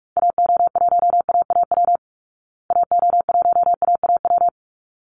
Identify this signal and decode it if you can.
Morse code, heard as a keyed tone, about 44 wpm, 700 Hz. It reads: AO1AAW AO1AAW
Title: Call Signs Repeated 3x - 45wpm